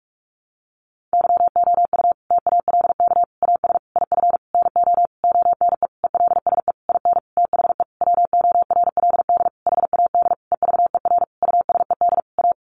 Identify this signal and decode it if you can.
Morse code, heard as a keyed tone, about 44 wpm, 700 Hz. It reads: YOU TALK AS IF NO ONE ELSE IN THE WORLD HAD EVER USED A
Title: Sentences from Top 1100 Words 45wpm